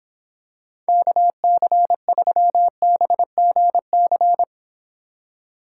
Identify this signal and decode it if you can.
Morse code, heard as a keyed tone, about 26 wpm, 700 Hz. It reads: KC3BGC